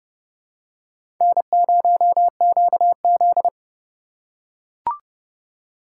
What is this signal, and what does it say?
Morse code, heard as a keyed tone, about 32 wpm, 700 Hz. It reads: N0QZ
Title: USA Call Signs - 4 Characters - 2x Character Spacing - 30wpm